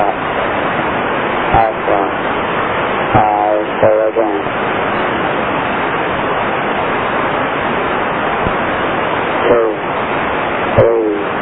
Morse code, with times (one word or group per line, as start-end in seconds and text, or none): none